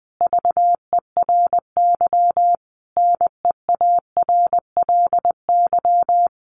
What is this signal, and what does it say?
Morse code, heard as a keyed tone, about 19 wpm, 700 Hz. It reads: VERY NEARLY
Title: Sentences from Top 1400 Words 20wpm